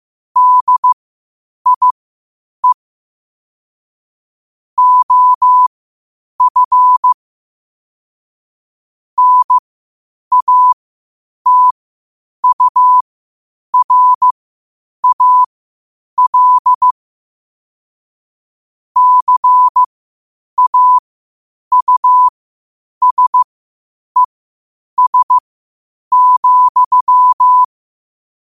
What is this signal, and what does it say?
Morse code, heard as a keyed tone, about 15 wpm, spaced as for 8 wpm, 1000 Hz. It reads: DIE OF NATURAL CAUSES,
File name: qotdfiles / 8wpm